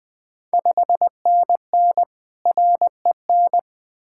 Morse code, heard as a keyed tone, about 20 wpm, 700 Hz. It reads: 5NN REN
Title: Ontario QSO Party Exchange 20wpm